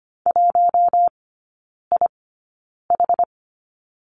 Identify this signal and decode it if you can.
Morse code, heard as a keyed tone, about 25 wpm, spaced as for 9 wpm, 690 Hz. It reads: 1IH
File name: Lesson 14e - Number 2 - Three Characters 25wpm